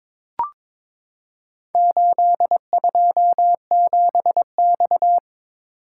Morse code, heard as a keyed tone, about 22 wpm, 700 Hz. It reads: 827X
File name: practice / 22wpm / Four Letter-Number Combinations - 22wpm